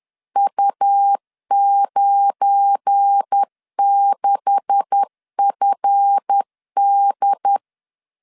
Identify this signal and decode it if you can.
Morse code, heard as a keyed tone, about 10 wpm, 800 Hz. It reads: U96FD
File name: LEZIONI / 10 SIGNS / 10B INT 10wpm 2844